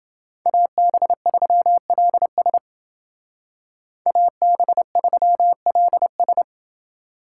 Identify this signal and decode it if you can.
Morse code, heard as a keyed tone, about 28 wpm, 700 Hz. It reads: AB3LS AB3LS